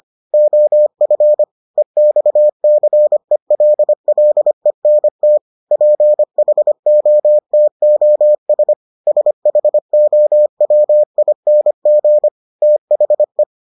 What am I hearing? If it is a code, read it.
Morse code, heard as a keyed tone, about 25 wpm, 600 Hz. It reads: OF EXCELLENT PHOTOS SHOWING THE